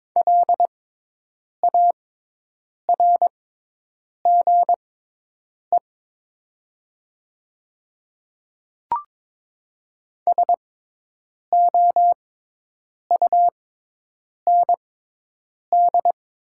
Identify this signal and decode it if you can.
Morse code, heard as a keyed tone, about 22 wpm, spaced as for 8 wpm, 700 Hz. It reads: LARGE SOUND
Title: 6x Character Spacing - Top 200 Words - 22wpm